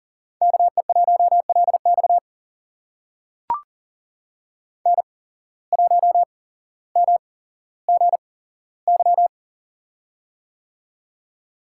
Morse code, heard as a keyed tone, about 40 wpm, spaced as for 17 wpm, 700 Hz. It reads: KE1LX N1MGY
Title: USA Call Signs - 5 Characters - 8x Character Spacing - 40wpm